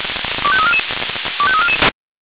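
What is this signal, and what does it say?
Five-tone selective call (ZVEI), heard as quick runs of five tones, tones of about 80 ms. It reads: 25344 25344
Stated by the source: ZVEI five-tone selective call received on an SDR.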